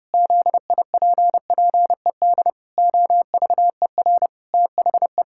Morse code, heard as a keyed tone, about 30 wpm, 700 Hz. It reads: ZIPPED OVER THE
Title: Pangram Challenge v2 - 30wpm